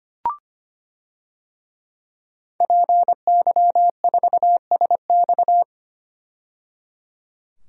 Morse code, heard as a keyed tone, about 25 wpm, 700 Hz. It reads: PY4SX
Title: DXCC-100 Call Signs - 5 Characters - 25wpm